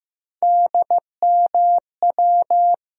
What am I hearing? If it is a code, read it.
Morse code, heard as a keyed tone, about 15 wpm, 700 Hz. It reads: DMW
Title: USA Call Sign Postfixes - 3 Letters 15wpm